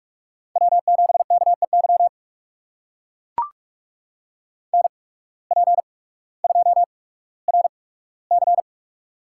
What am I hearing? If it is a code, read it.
Morse code, heard as a keyed tone, about 44 wpm, spaced as for 19 wpm, 700 Hz. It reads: W7KEY NP2RC
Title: USA Call Signs - 5 Characters - 8x Character Spacing - 45wpm